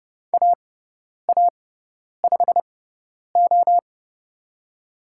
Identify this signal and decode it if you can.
Morse code, heard as a keyed tone, about 30 wpm, spaced as for 10 wpm, 700 Hz. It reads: AA5O
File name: Lesson 16g - Letter c - Call Signs 30wpm